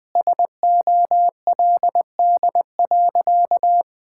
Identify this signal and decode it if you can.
Morse code, heard as a keyed tone, about 21 wpm, 700 Hz. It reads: SOLD.
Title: Sentences with 4-Word Phrases - 20wpm